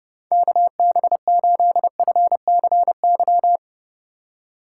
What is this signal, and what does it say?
Morse code, heard as a keyed tone, about 32 wpm, 700 Hz. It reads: KB8FCY
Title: USA Call Signs - 2x Character Spacing - 30wpm